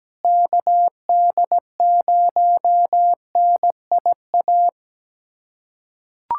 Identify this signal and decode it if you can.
Morse code, heard as a keyed tone, about 17 wpm, 700 Hz. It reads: KD0NIA
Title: USA Callsign DE Callsign 17wpm